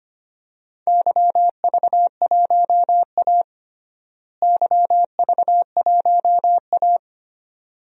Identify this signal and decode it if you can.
Morse code, heard as a keyed tone, about 25 wpm, 700 Hz. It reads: YV1A YV1A
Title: DXCC-100 Call Signs - 4 Characters - Repeated 2x - 25wpm